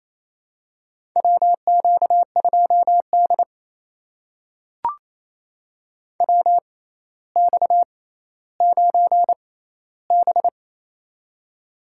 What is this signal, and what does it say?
Morse code, heard as a keyed tone, about 28 wpm, spaced as for 14 wpm, 700 Hz. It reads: WQ2D WX9B
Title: USA Call Signs - 4 Characters - 6x Character Spacing - 28wpm